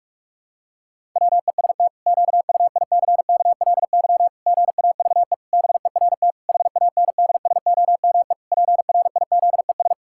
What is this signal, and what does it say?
Morse code, heard as a keyed tone, about 46 wpm, 700 Hz. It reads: WEST QUICKLY GAVE BERT HANDSOME PRIZES
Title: Pangram Challenge v2 - 45wpm